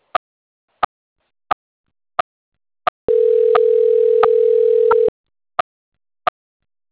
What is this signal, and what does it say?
Ringback tone (North America), 440 + 480 Hz.